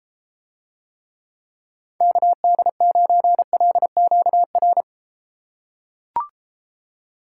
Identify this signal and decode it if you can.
Morse code, heard as a keyed tone, about 31 wpm, 700 Hz. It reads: KD9LQR